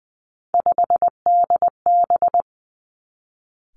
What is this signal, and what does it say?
Morse code, heard as a keyed tone, about 20 wpm, 700 Hz. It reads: 5DB
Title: Three Challenging Letter-Number Combinations 20wpm